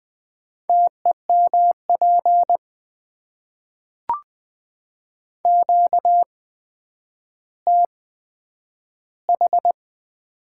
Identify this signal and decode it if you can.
Morse code, heard as a keyed tone, about 20 wpm, spaced as for 10 wpm, 700 Hz. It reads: TEMP QTH
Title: Top 90 QSO Elements - 8x Character Spacing - 20wpm